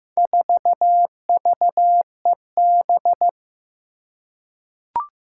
Morse code, heard as a keyed tone, about 15 wpm, 700 Hz. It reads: 4VEB
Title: Four Challenging Letter-Number Combinations 15wpm